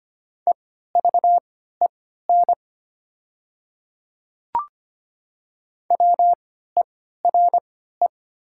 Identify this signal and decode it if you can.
Morse code, heard as a keyed tone, about 25 wpm, spaced as for 14 wpm, 700 Hz. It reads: EVEN WERE
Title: 3x Character Spacing - Top 200 Words - 25wpm